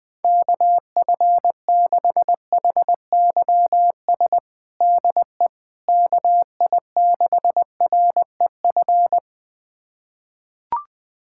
Morse code, heard as a keyed tone, about 21 wpm, 700 Hz. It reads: KF6HYS DE KI6REF
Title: USA Callsign DE Callsign - 6 Characters - 20wpm